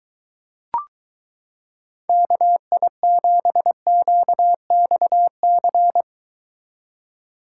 Morse code, heard as a keyed tone, about 23 wpm, 700 Hz. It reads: KI7QXC